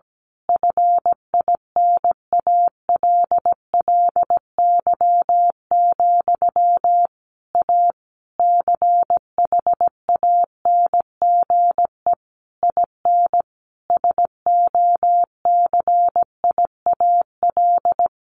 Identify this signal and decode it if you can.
Morse code, heard as a keyed tone, about 17 wpm, 700 Hz. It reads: FINALLY, A CHANGE IN SOCIAL
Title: A Sound Of Thunder 17wpm